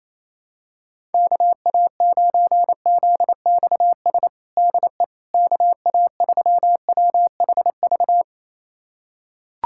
Morse code, heard as a keyed tone, about 27 wpm, 700 Hz. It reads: KA9ZXS DE KA3WHV
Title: USA Callsign DE Callsign - 6 Characters - 28wpm